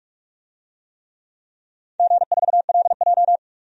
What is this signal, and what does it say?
Morse code, heard as a keyed tone, about 41 wpm, 700 Hz. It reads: G4LJ
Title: DXCC-100 Call Signs - 4 Characters - 45wpm